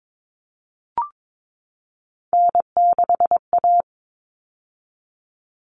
Morse code, heard as a keyed tone, about 22 wpm, 700 Hz. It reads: N6A
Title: All Cut Numbers - 3-Digit 22wpm